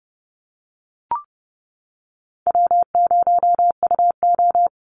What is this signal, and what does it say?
Morse code, heard as a keyed tone, about 28 wpm, 700 Hz. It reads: W0UO
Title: Call Signs - 30wpm